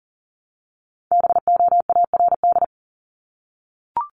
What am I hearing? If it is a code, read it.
Morse code, heard as a keyed tone, about 40 wpm, 700 Hz. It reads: BOARD
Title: Top 501-750 Words 40wpm